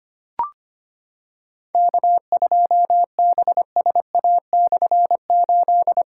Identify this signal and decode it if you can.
Morse code, heard as a keyed tone, about 25 wpm, 700 Hz. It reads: K2BSA/8
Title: K1USN SST QSO Practice - 25wpm